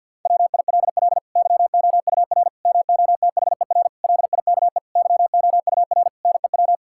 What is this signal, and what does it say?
Morse code, heard as a keyed tone, about 50 wpm, 700 Hz. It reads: WILL YOUR MOTHER LIKE YOUR NEW
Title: Sentences from Top 1000 Words - Review Part 5 - 50wpm